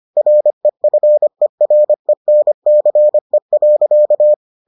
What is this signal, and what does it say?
Morse code, heard as a keyed tone, about 25 wpm, 600 Hz. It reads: REFERENCE.